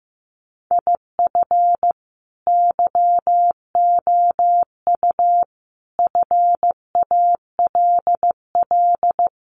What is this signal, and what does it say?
Morse code, heard as a keyed tone, about 15 wpm, 700 Hz. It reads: IF YOU FALL
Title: Sentences from Top 500 Words 15wpm